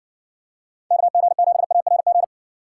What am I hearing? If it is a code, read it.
Morse code, heard as a keyed tone, about 60 wpm, 700 Hz. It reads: KG7ARG